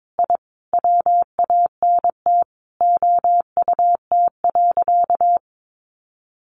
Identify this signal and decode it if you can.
Morse code, heard as a keyed tone, about 22 wpm, 700 Hz. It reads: I WANT OUT.
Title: A Sound Of Thunder 22wpm